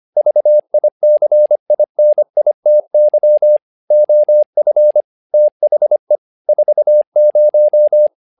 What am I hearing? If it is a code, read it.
Morse code, heard as a keyed tone, about 25 wpm, 600 Hz. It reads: VICINITY OF THE 40
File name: learning center / tower trouble / tower trouble 25wpm